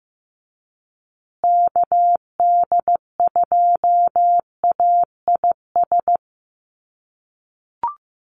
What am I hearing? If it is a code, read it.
Morse code, heard as a keyed tone, about 15 wpm, 700 Hz. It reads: KD2AIS
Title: USA Call Signs Repeated 2x 15wpm